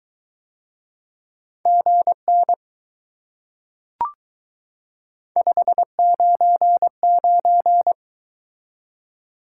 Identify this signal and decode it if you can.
Morse code, heard as a keyed tone, about 22 wpm, 700 Hz. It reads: GN 599